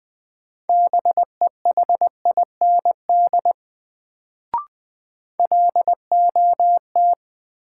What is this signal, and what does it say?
Morse code, heard as a keyed tone, about 20 wpm, 700 Hz. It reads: BEHIND LOT
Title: Sentences from Top 500 Words - New Word Drill - 20wpm